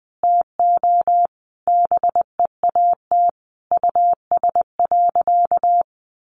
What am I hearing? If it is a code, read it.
Morse code, heard as a keyed tone, about 19 wpm, 700 Hz. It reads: TO BEAT US.
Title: Sentences from Top 1300 Words 20wpm